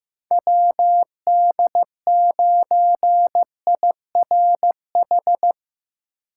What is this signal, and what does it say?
Morse code, heard as a keyed tone, about 15 wpm, 700 Hz. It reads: WD9IRH
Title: USA Call Signs - 6 Characters - 4x Character Spacing - 15wpm